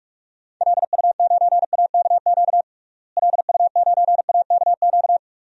Morse code, heard as a keyed tone, about 46 wpm, 700 Hz. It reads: LU9AKQ LU9AKQ
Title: DXCC-100 Call Signs - 6 Characters - Repeated 3x - 45wpm